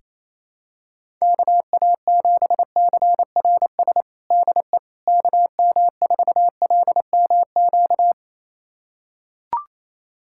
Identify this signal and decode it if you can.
Morse code, heard as a keyed tone, about 28 wpm, 700 Hz. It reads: KA7CRS DE KM4LMQ
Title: USA Callsign DE Callsign 28wpm